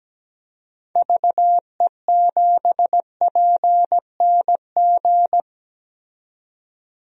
Morse code, heard as a keyed tone, about 17 wpm, 700 Hz. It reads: VE7PNG
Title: DXCC-100 Call Signs - 6 Characters - 17wpm